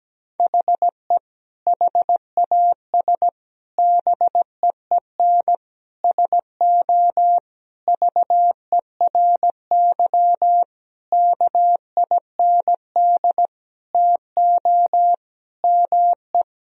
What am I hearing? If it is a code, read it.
Morse code, heard as a keyed tone, about 17 wpm, 700 Hz. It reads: HE HAS BEEN SO VERY KIND TO ME
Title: Sentences from Top 1000 Words - Review Part 1 - 17wpm